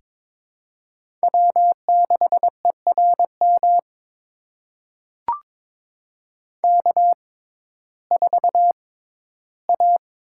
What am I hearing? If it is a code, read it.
Morse code, heard as a keyed tone, about 21 wpm, spaced as for 13 wpm, 700 Hz. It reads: W6ERM K4A
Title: USA Call Signs - 5 Characters - 6x Character Spacing - 22wpm